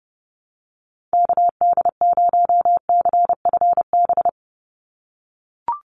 Morse code, heard as a keyed tone, about 30 wpm, 700 Hz. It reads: KD0CFB
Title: USA Call Signs 30wpm